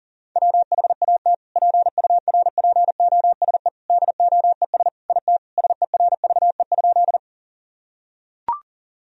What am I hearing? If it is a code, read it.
Morse code, heard as a keyed tone, about 37 wpm, 700 Hz. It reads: WHAT PURPOSE DOES IT SERVE?
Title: Sentences from Top 1000 Words - Review Part 2 - 40wpm